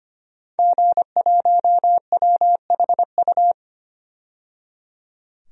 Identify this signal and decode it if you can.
Morse code, heard as a keyed tone, about 25 wpm, 700 Hz. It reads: G1WHU
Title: DXCC-100 Call Signs - 5 Characters - 25wpm